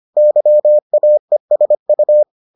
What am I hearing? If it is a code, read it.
Morse code, heard as a keyed tone, about 25 wpm, 600 Hz. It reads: YAESU